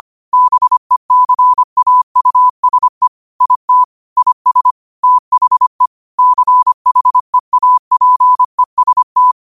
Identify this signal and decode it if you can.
Morse code, heard as a keyed tone, about 25 wpm, 1000 Hz. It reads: BECAUSE IT IS THE CHEAPEST
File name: qotdfiles / 25wpm